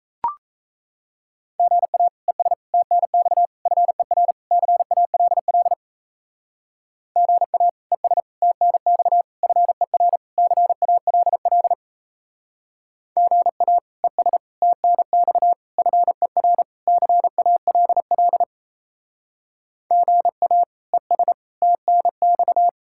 Morse code, heard as a keyed tone, about 37 wpm, 700 Hz. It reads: GA ES TNX FER CALL GA ES TNX FER CALL GA ES TNX FER CALL GA ES TNX